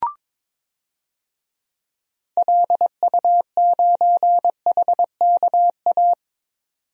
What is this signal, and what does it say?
Morse code, heard as a keyed tone, about 22 wpm, 700 Hz. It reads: LU9HKA
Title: DXCC-100 Call Signs - 6 Characters - 22wpm